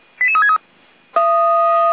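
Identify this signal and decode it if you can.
Five-tone selective call (ZVEI), heard as quick runs of five tones, tones of about 70 ms. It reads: 88363
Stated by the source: ZVEI five-tone selective call received on an SDR.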